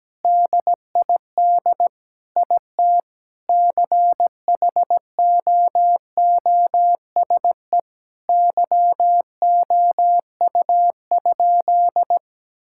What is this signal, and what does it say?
Morse code, heard as a keyed tone, about 17 wpm, 700 Hz. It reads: DID IT CHOOSE YOU?
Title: Sentences from Top 1000 Words - Review Part 6 - 17wpm